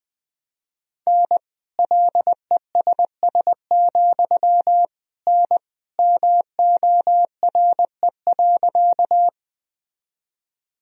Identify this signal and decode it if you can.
Morse code, heard as a keyed tone, about 19 wpm, 700 Hz. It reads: N LESS, N MORE.
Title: Sentences from Top 400 Words and Abbreviations 20wpm